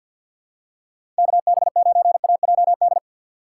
Morse code, heard as a keyed tone, about 49 wpm, 700 Hz. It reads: KB9AJD
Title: USA Call Signs - 3x Character Spacing - 50wpm